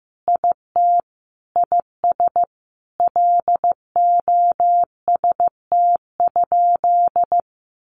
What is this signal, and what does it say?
Morse code, heard as a keyed tone, about 14 wpm, 700 Hz. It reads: IT IS LOST?
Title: Sentences from Top 1400 Words 15wpm